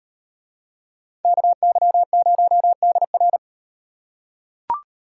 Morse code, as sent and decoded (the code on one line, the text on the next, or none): -.- -.-- ----- -.. .-.
KY0DR